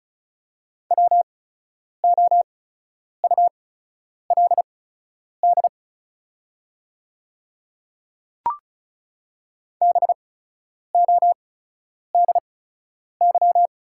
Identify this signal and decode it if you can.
Morse code, heard as a keyed tone, about 36 wpm, spaced as for 10 wpm, 690 Hz. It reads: WOULD BODY
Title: 8x Character Spacing - Top 500 Words - 35wpm